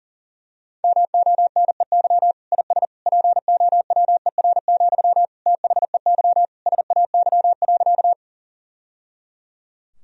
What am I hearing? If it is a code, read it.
Morse code, heard as a keyed tone, about 40 wpm, 700 Hz. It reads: MONEY IS POWER, THEY SAY.
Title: Sentences from Top 1000 Words - Review Part 2 - 40wpm